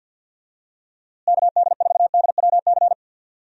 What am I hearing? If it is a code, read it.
Morse code, heard as a keyed tone, about 51 wpm, 700 Hz. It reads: KD4DWC
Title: USA Call Signs - 6 Characters - Repeated 3x - 50wpm